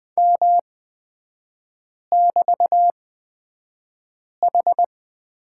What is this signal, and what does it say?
Morse code, heard as a keyed tone, about 20 wpm, spaced as for 5 wpm, 700 Hz. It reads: M<BT>H
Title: 224 Lesson 36e - BT - Three Characters 20wpm